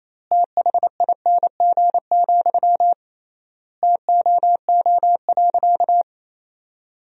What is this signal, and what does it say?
Morse code, heard as a keyed tone, about 28 wpm, 700 Hz. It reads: THING, TOO.
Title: 3x Word-Spacing - Sentences from Top 200 Words - 28wpm